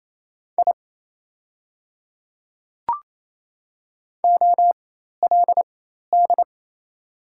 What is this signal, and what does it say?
Morse code, heard as a keyed tone, about 28 wpm, spaced as for 13 wpm, 700 Hz. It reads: I OLD
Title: 4x Character Spacing - Top 300 Words - 28wpm